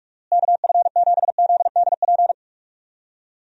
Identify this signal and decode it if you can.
Morse code, heard as a keyed tone, about 44 wpm, 700 Hz. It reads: KF7ZDP
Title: USA Call Signs - 6 Characters - Repeated 3x - 45wpm